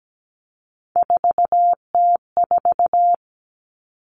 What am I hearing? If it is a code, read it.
Morse code, heard as a keyed tone, about 17 wpm, 700 Hz. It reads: 4T4
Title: Common Cut Numbers - 3-Digit 17wpm